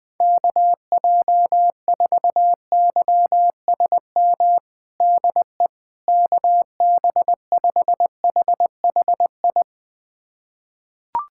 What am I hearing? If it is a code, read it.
Morse code, heard as a keyed tone, about 21 wpm, 700 Hz. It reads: KJ4YSM DE KB5HHI
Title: USA Callsign DE Callsign - 6 Characters - 20wpm